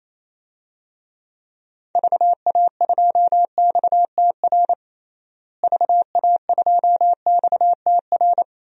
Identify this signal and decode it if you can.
Morse code, heard as a keyed tone, about 28 wpm, 700 Hz. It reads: VA2XTR VA2XTR
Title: DXCC-100 Call Signs - 6 Characters - Repeated 3x - 28wpm